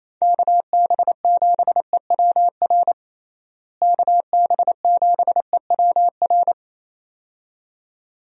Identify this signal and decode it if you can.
Morse code, heard as a keyed tone, about 28 wpm, 700 Hz. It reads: KB7EWR KB7EWR
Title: USA Call Signs - 6 Characters - Repeated 2x - 28wpm